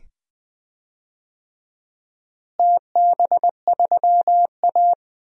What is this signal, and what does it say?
Morse code, heard as a keyed tone, about 20 wpm, 700 Hz. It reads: TB3A
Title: DXCC-100 Call Signs - 4 Characters - Repeated 2x - 20wpm